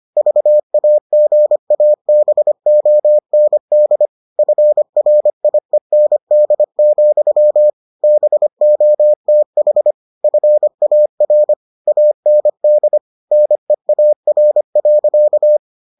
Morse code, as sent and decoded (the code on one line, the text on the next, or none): ...- .- --. .- -... --- -. -.. / ..-. .-. .. -. -.. --..-- / -... --- - .... / ..-. .- .-. / .- -. -.. / -. .- .-. .-.-.-
VAGABOND FRIEND, BOTH FAR AND NEAR.